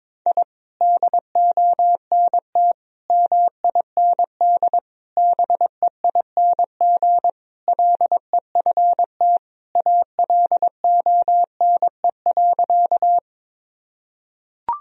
Morse code, heard as a keyed tone, about 22 wpm, 700 Hz. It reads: I DONT MIND BEING LEFT ALONE.
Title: Sentences from Top 1000 Words - Review Part 3 - 22wpm